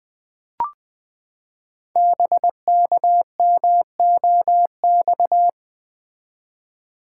Morse code, heard as a keyed tone, about 20 wpm, 700 Hz. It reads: BKMOX
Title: Code Groups - 5 Letters - 20wpm